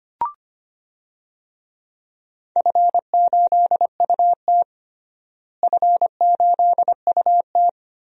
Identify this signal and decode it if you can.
Morse code, heard as a keyed tone, about 25 wpm, 700 Hz. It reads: F8UT F8UT
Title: DXCC-100 Call Signs - 4 Characters - Repeated 3x - 25wpm